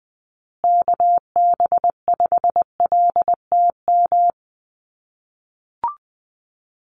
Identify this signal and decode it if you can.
Morse code, heard as a keyed tone, about 20 wpm, 700 Hz. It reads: KB5LTM
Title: USA Call Signs Repeated 2x 20wpm